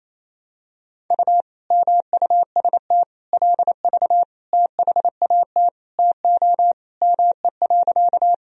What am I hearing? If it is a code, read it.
Morse code, heard as a keyed tone, about 28 wpm, 700 Hz. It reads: U MUST LV THAT TO ME.
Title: Sentences from Top 400 Words and Abbreviations 28wpm